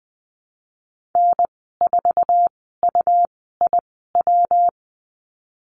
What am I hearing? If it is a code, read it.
Morse code, heard as a keyed tone, about 20 wpm, spaced as for 15 wpm, 700 Hz. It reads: N4UIW
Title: 154 Lesson 25g - Letter p - Call Signs 20wpm